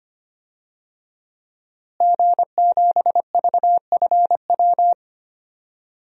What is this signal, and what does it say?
Morse code, heard as a keyed tone, about 25 wpm, 700 Hz. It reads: G7VFW
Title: DXCC-100 Call Signs - 5 Characters - Repeated 3x - 25wpm